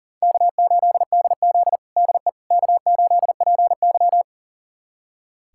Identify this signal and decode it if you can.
Morse code, heard as a keyed tone, about 37 wpm, 700 Hz. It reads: K8DZ DE K8PY
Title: USA Callsign DE Callsign - 4 Characters - 40wpm